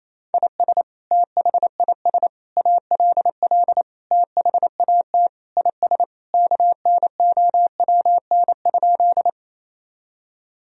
Morse code, heard as a keyed tone, about 28 wpm, 700 Hz. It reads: IS THIS ALL THAT IS KNOWN?
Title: Sentences from Top 1000 Words - Review Part 3 - 28wpm